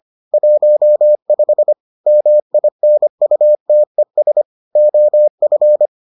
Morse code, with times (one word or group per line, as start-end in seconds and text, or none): 0.3-1.7 15
2.1-4.4 MINUTES
4.8-5.9 OF